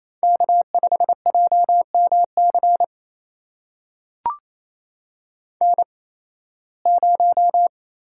Morse code, heard as a keyed tone, about 28 wpm, spaced as for 18 wpm, 700 Hz. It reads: K5JMC N0
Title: USA Call Signs - 8x Character Spacing - 28wpm